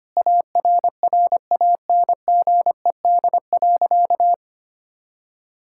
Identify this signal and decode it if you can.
Morse code, heard as a keyed tone, about 25 wpm, 700 Hz. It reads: ARRANGED.
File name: practice / 25wpm / Sentences with 3-Word Phrases - 25wpm